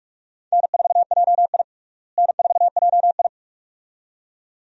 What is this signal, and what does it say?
Morse code, heard as a keyed tone, about 46 wpm, 700 Hz. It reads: N4JI N4JI
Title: USA Call Signs - 4 Characters - Repeated 2x - 45wpm